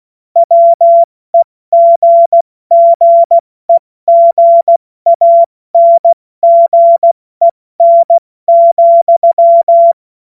Morse code, heard as a keyed tone, about 16 wpm, 680 Hz. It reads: WEGGEGANGEN,
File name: Morsekurs / Witze / Witz 16. Dezember 2025 70 F80